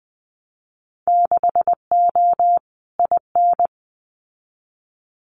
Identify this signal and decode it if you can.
Morse code, heard as a keyed tone, about 20 wpm, 700 Hz. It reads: 6O IN
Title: Winter Field Day Exchanges Encore 20wpm